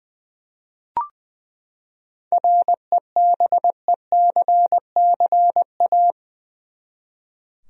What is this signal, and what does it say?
Morse code, heard as a keyed tone, about 20 wpm, 700 Hz. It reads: REBECCA
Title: USA Common Names - 20wpm